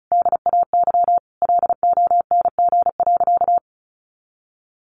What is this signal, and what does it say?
Morse code, heard as a keyed tone, about 34 wpm, 700 Hz. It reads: DAY LONG.
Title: Sentences from Top 1400 Words 35wpm